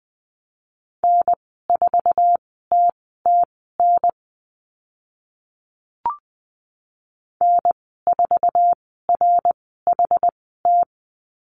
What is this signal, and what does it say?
Morse code, heard as a keyed tone, about 20 wpm, spaced as for 14 wpm, 700 Hz. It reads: N4TTN N4RHT
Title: 076 Lesson 13g - Letter L - Call Signs 20wpm